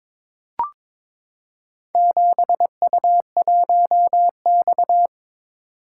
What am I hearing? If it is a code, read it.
Morse code, heard as a keyed tone, about 22 wpm, 700 Hz. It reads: 7U1X
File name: practice / 22wpm / Four Letter-Number Combinations - 22wpm